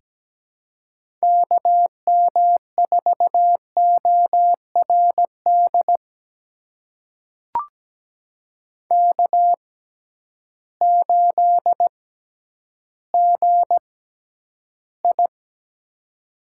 Morse code, heard as a keyed tone, about 17 wpm, spaced as for 10 wpm, 700 Hz. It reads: KM4ORD K8GI
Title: USA Call Signs - 6x Character Spacing - 17wpm